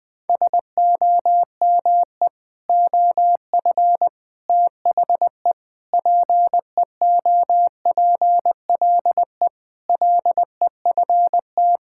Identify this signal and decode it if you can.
Morse code, heard as a keyed tone, about 21 wpm, 700 Hz. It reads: SOME OF THE PEOPLE LEFT
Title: Sentences with 3-Word Phrases - 20wpm